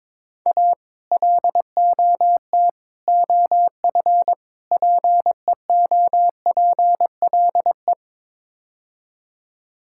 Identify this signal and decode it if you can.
Morse code, heard as a keyed tone, about 22 wpm, 700 Hz. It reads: A LOT OF PEOPLE
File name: practice / 22wpm / Build A Phrase - 5-Word Phrases - 22wpm